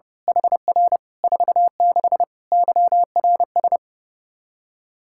Morse code, heard as a keyed tone, about 30 wpm, 700 Hz. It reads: HR 46 YRS
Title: Practice QSO - Exchange 3 - More Info - 30wpm